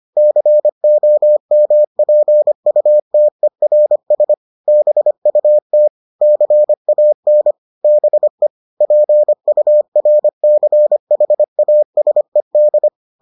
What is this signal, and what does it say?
Morse code, heard as a keyed tone, about 25 wpm, 600 Hz. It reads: COMPUTERS BUT CAN BE PURCHASED